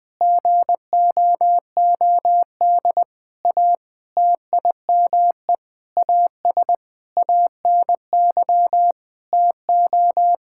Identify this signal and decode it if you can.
Morse code, heard as a keyed tone, about 20 wpm, 700 Hz. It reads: GOOD A TIME AS ANY TO
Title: Sentences with 8-Word Phrases - 20wpm